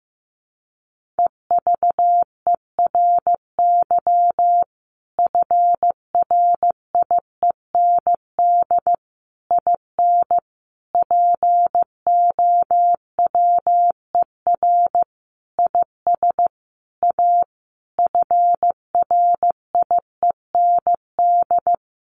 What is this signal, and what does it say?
Morse code, heard as a keyed tone, about 15 wpm, 700 Hz. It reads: EVERY FRIEND IN POWER IS A FRIEND
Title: Sentences from Top 1400 Words 15wpm